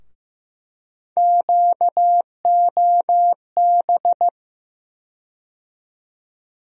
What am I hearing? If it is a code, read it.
Morse code, heard as a keyed tone, about 15 wpm, 700 Hz. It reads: QOB
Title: USA Call Sign Postfixes - 3 Letters 15wpm